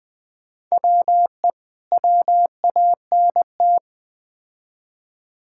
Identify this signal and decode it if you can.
Morse code, heard as a keyed tone, about 20 wpm, 700 Hz. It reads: WE WANT
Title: Build A Phrase - 3-Word Phrases - 20wpm